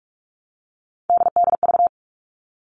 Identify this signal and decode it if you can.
Morse code, heard as a keyed tone, about 44 wpm, 700 Hz. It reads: DDV
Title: All Cut Numbers - 3-Digit 45wpm